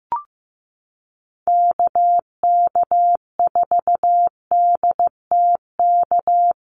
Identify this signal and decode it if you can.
Morse code, heard as a keyed tone, about 15 wpm, 700 Hz. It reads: KK4DTK
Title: USA Call Signs Repeated 2x 15wpm